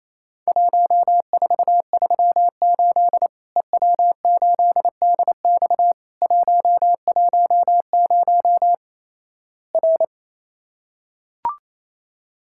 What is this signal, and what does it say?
Morse code, heard as a keyed tone, about 28 wpm, 700 Hz. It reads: 1438 EW8DX 110 R
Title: Worked All Europe DX Contest - QTC Exchange - 28wpm